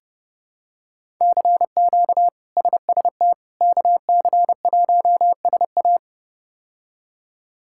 Morse code, heard as a keyed tone, about 30 wpm, 700 Hz. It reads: CQ SST KC1SA
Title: K1USN SST QSO Practice - 30wpm